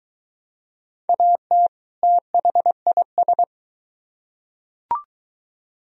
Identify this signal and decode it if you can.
Morse code, heard as a keyed tone, about 22 wpm, 700 Hz. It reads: AT THIS